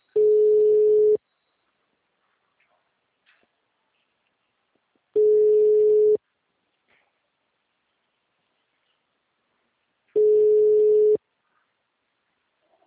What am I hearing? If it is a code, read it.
Ringback tone (Europe, 425 Hz).